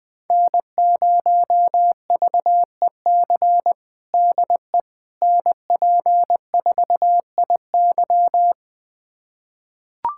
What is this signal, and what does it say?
Morse code, heard as a keyed tone, about 20 wpm, 700 Hz. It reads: N0VEC DE NP4IY
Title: USA Callsign DE Callsign - 5 Characters - 20wpm